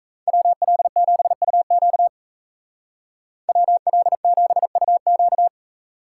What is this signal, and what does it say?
Morse code, heard as a keyed tone, about 37 wpm, 700 Hz. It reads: WL7UQ WL7UQ